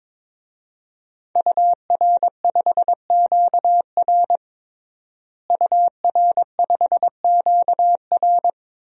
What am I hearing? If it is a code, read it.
Morse code, heard as a keyed tone, about 22 wpm, 700 Hz. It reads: UR5QR UR5QR
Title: DXCC-100 Call Signs - 5 Characters - Repeated 3x - 22wpm